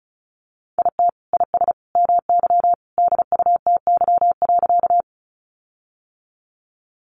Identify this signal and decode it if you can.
Morse code, heard as a keyed tone, about 34 wpm, 700 Hz. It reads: IT IS MY DUTY.
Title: Sentences from Top 1500 Words 35wpm